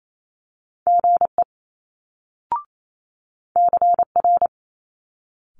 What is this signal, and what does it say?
Morse code, heard as a keyed tone, about 28 wpm, 700 Hz. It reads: GE CR
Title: Multi-letter Phonemes Quick Fire 28wpm